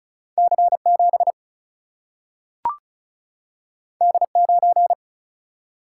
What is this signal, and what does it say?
Morse code, heard as a keyed tone, about 36 wpm, 700 Hz. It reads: C7 D9
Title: Single-Letter Single-Number 35wpm